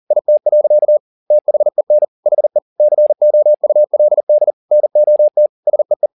Morse code, heard as a keyed tone, about 40 wpm, 600 Hz. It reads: IT. THEN HE COULD NOT SEE
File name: the old man and the sea 2 40wpm